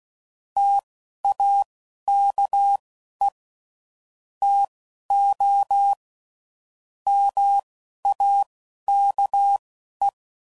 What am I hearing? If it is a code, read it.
Morse code, heard as a keyed tone, about 16 wpm, spaced as for 11 wpm, 800 Hz. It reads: TAKE TO MAKE
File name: CW Code Groups @10WPM / Technical Subjects / OR Gate